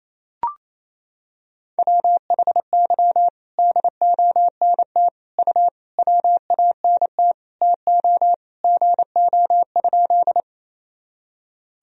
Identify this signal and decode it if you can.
Morse code, heard as a keyed tone, about 28 wpm, 700 Hz. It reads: WHY DONT U WANT TO GO?
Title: Sentences from Top 300 Words and Abbreviations 28wpm